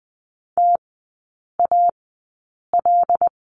Morse code, heard as a keyed tone, about 20 wpm, spaced as for 8 wpm, 700 Hz. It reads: TAL
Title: 107 Lesson 18e - Letter m - Three Characters 20wpm